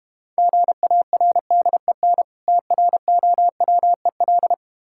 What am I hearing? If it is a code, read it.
Morse code, heard as a keyed tone, about 32 wpm, 700 Hz. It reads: GARDEN TROWEL